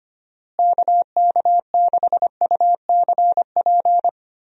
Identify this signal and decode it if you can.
Morse code, heard as a keyed tone, about 25 wpm, 700 Hz. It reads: KK6UCP
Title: USA Call Signs - 6 Characters - Repeated 2x - 25wpm